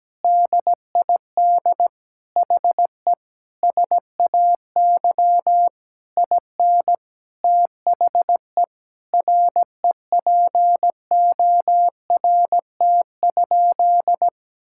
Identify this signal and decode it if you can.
Morse code, heard as a keyed tone, about 17 wpm, 700 Hz. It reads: DID HE SAY IN THE REPORT?
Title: Sentences from Top 1000 Words - Review Part 6 - 17wpm